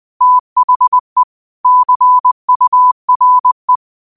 Morse code, heard as a keyed tone, about 20 wpm, 1000 Hz. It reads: THE CURE